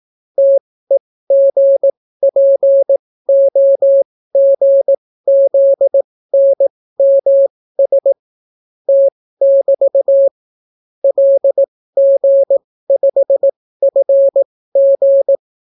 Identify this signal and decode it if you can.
Morse code, heard as a keyed tone, about 18 wpm, spaced as for 14 wpm, 550 Hz. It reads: TEGPOGZNMS T<BT> LG5FG